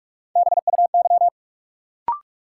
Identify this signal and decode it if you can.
Morse code, heard as a keyed tone, about 46 wpm, 700 Hz. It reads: BUY
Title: Sentences from Top 700 Words - New Word Drill - 45wpm